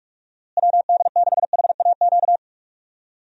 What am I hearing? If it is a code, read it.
Morse code, heard as a keyed tone, about 46 wpm, 700 Hz. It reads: WD6HAQ
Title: USA Call Signs - 6 Characters - Repeated 2x - 45wpm